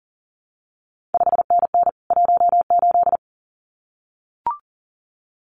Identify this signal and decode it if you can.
Morse code, heard as a keyed tone, about 38 wpm, 700 Hz. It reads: 5NN 18
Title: CQ WW DX Contest Exchange Encore - 40wpm